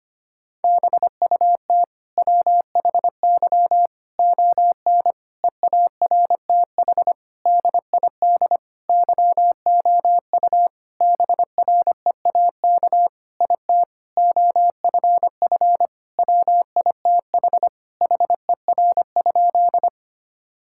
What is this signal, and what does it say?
Morse code, heard as a keyed tone, about 25 wpm, 700 Hz. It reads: BUT WHY ON EARTH DID YOU BREAK IT OFF WITH HER?
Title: Sentences from Top 1000 Words - Review Part 6 - 25wpm